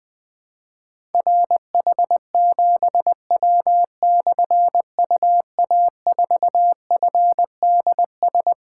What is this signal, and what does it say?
Morse code, heard as a keyed tone, about 20 wpm, 700 Hz. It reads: RH7W/UA4FDS
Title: Difficult Call Signs 20wpm